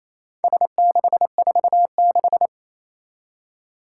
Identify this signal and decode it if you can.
Morse code, heard as a keyed tone, about 29 wpm, 700 Hz. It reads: S646
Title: Four Challenging Letter-Number Combinations 28wpm